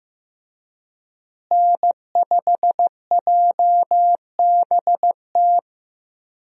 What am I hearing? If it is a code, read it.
Morse code, heard as a keyed tone, about 16 wpm, 700 Hz. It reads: N5JBT
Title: USA Call Signs - 5 Characters - Repeated 3x - 15wpm